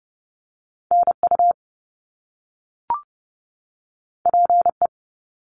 Morse code, heard as a keyed tone, about 30 wpm, 700 Hz. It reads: NU PE
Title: Canadian Provinces and Territories 30wpm